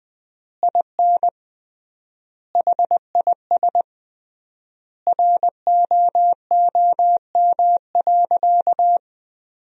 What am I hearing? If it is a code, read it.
Morse code, heard as a keyed tone, about 20 wpm, 700 Hz. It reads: IN HIS ROOM.
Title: 3x Word-Spacing - Sentences from Top 300 Words - 20wpm